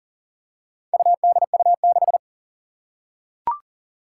Morse code, heard as a keyed tone, about 40 wpm, 700 Hz. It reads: UDU6
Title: Four Challenging Letter-Number Combinations 40wpm